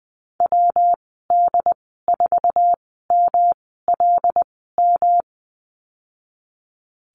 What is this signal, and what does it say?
Morse code, heard as a keyed tone, about 20 wpm, spaced as for 15 wpm, 700 Hz. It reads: WD4MLM
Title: 147 Lesson 24g - Letter y - Call Signs 20wpm